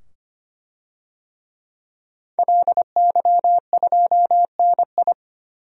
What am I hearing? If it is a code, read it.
Morse code, heard as a keyed tone, about 25 wpm, 700 Hz. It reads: LY2NI
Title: DXCC-100 Call Signs - 5 Characters - 25wpm